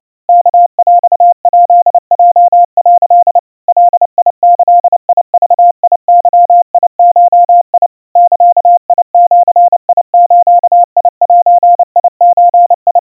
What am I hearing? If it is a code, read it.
Morse code (Japanese Wabun code), heard as a keyed tone, about 29 wpm, 700 Hz. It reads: ワヰヱヲ」 ガギグゲゴ ザジズゼゾ
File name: aiueo 100 cpm 700 Hz Crispy